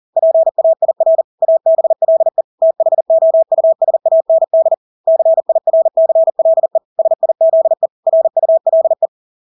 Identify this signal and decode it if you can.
Morse code, heard as a keyed tone, about 40 wpm, 650 Hz. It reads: PAIR ABLE THOUSAND CIRCLE SIZE RULE
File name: Engelske ord 40-40 wpm